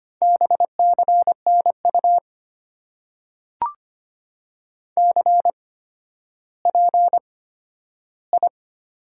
Morse code, heard as a keyed tone, about 25 wpm, spaced as for 12 wpm, 690 Hz. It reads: BCNU CPI
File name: Top 90 QSO Elements - 8x Character Spacing - 25wpm